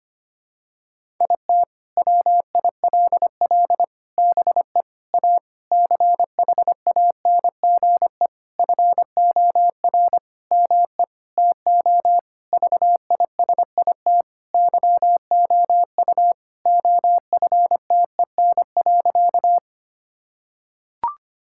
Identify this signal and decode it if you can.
Morse code, heard as a keyed tone, about 25 wpm, 700 Hz. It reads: IT WILL BE A CHANGE FOR ME TO VISIT YOU OFTEN.
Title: Sentences from Top 1000 Words - Review Part 6 - 25wpm